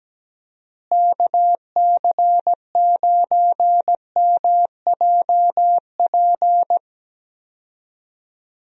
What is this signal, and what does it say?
Morse code, heard as a keyed tone, about 17 wpm, 700 Hz. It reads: KC9MJP
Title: USA Call Signs - 6 Characters - Repeated 2x - 17wpm